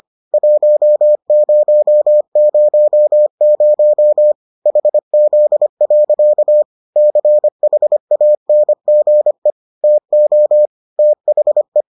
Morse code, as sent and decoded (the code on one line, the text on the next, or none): .---- ----- ----- ----- / .... --.. .-.-.- / -.-. .... .- -. --. / - --- / - .... .
1000 HZ. CHANGE TO THE